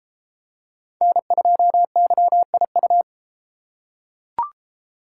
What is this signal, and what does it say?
Morse code, heard as a keyed tone, about 33 wpm, 700 Hz. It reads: N2YIU